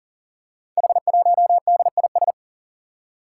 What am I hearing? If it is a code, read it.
Morse code, heard as a keyed tone, about 38 wpm, 700 Hz. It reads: H1DIS
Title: Five Challenging Letter-Number Combinations 40wpm